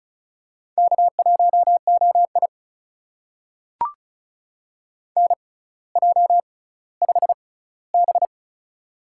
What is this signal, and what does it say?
Morse code, heard as a keyed tone, about 36 wpm, spaced as for 18 wpm, 700 Hz. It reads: K1OI NJ5B
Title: USA Call Signs - 4 Characters - 6x Character Spacing - 35wpm